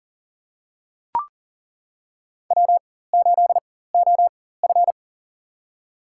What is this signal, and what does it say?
Morse code, heard as a keyed tone, about 40 wpm, spaced as for 19 wpm, 700 Hz. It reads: W8OF
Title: USA Call Signs - 4 Characters - 4x Character Spacing - 40wpm